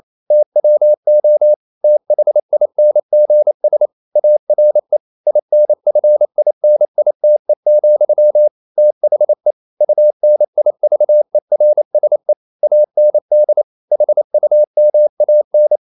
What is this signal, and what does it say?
Morse code, heard as a keyed tone, about 28 wpm, 600 Hz. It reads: TWO THINGS ARE INFINITE, THE UNIVERSE AND HUMAN
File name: CW / Practice / EnglishQuotations0128